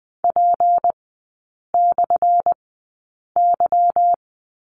Morse code, heard as a keyed tone, about 20 wpm, spaced as for 8 wpm, 700 Hz. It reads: P/Y
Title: 178 Lesson 29e - Slash - Three Characters 20wpm